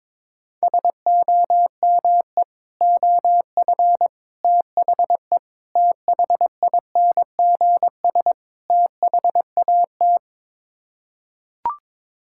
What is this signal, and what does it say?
Morse code, heard as a keyed tone, about 22 wpm, 700 Hz. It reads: SOME OF THE THINGS THAT
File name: practice / 22wpm / Build A Phrase - 5-Word Phrases - 22wpm